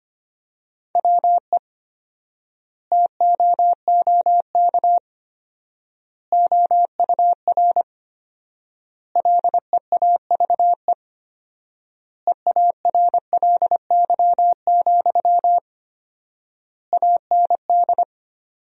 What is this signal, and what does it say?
Morse code, heard as a keyed tone, about 25 wpm, 700 Hz. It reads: WE TOOK OUR LEAVE EARLY, AND
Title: 4x Word-Spacing - Sentences from Top 400 Words - 25wpm